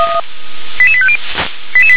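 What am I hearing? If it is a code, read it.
Five-tone selective call (ZVEI), heard as quick runs of five tones, tones of about 70 ms. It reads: 88850
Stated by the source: ZVEI five-tone selective call received with an SDR.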